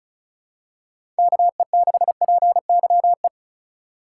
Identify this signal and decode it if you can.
Morse code, heard as a keyed tone, about 34 wpm, 700 Hz. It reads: KE6PYE
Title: USA Call Signs - 6 Characters - 4x Character Spacing - 35wpm